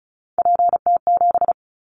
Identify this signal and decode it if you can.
Morse code, heard as a keyed tone, about 36 wpm, 700 Hz. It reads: PT7
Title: Callsign Prefixes for Top 50 DXCC - 35wpm